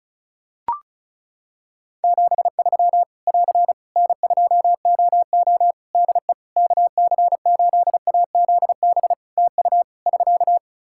Morse code, heard as a keyed tone, about 34 wpm, 700 Hz. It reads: 73 <AR> N2OO DE KC8AZB TU <SK>
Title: Practice QSO - Exchange 4 - Ending - 35wpm